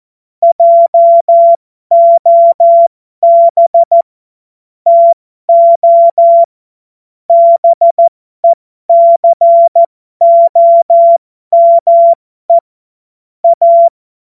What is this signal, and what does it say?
Morse code, heard as a keyed tone, about 14 wpm, spaced as for 12 wpm, 680 Hz. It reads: JOB TO BECOME A
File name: Morsekurs / Witze